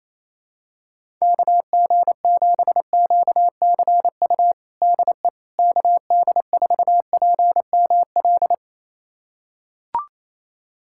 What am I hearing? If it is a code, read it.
Morse code, heard as a keyed tone, about 28 wpm, 700 Hz. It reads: KG7QCU DE KD4PML
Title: USA Callsign DE Callsign - 6 Characters - 28wpm